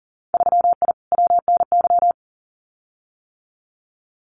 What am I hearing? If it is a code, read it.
Morse code, heard as a keyed tone, about 40 wpm, 700 Hz. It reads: 3I WNY
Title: Winter Field Day Exchanges 40wpm